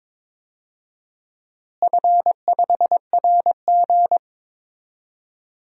Morse code, heard as a keyed tone, about 21 wpm, 700 Hz. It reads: F5RG
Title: DXCC-100 Call Signs - 4 Characters - Repeated 3x - 22wpm